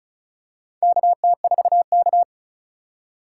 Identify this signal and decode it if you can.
Morse code, heard as a keyed tone, about 36 wpm, 700 Hz. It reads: KT4K
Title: USA Call Signs - 4 Characters - 4x Character Spacing - 35wpm